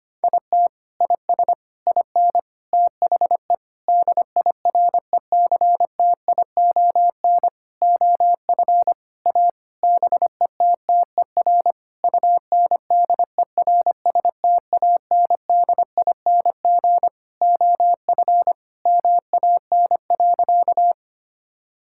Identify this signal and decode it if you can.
Morse code, heard as a keyed tone, about 25 wpm, 700 Hz. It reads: IT IS IN THE DIRECTION OF A BETTER UNDERSTANDING OF MAN.
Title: Sentences from Top 1000 Words - Review Part 5 - 25wpm